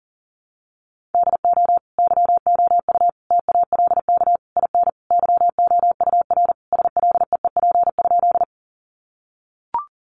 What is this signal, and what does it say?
Morse code, heard as a keyed tone, about 40 wpm, 700 Hz. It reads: DO YOU TALK IN YOUR SLEEP?
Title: Sentences from Top 500 Words 40wpm